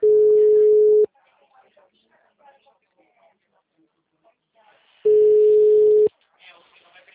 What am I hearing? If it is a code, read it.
Ringback tone (Europe, 425 Hz), with speech.